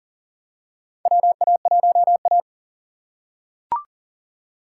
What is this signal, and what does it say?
Morse code, heard as a keyed tone, about 37 wpm, 700 Hz. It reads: WA1A
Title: USA Call Signs - 4 Characters - 6x Character Spacing - 40wpm